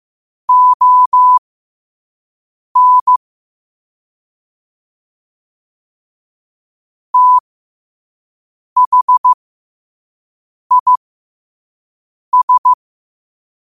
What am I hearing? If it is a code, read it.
Morse code, heard as a keyed tone, about 15 wpm, spaced as for 5 wpm, 1000 Hz. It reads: ON THIS
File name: qotdfiles / 5wpm